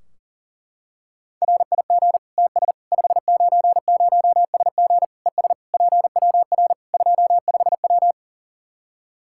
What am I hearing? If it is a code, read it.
Morse code, heard as a keyed tone, about 37 wpm, 700 Hz. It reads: RIG TS 590SG ES PWR 25W
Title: Practice QSO - Exchange 2 - Info - 40wpm